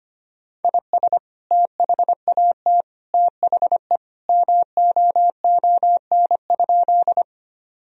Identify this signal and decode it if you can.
Morse code, heard as a keyed tone, about 25 wpm, 700 Hz. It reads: IS THAT THE MOON?
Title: Sentences from Top 1000 Words - Review Part 3 - 25wpm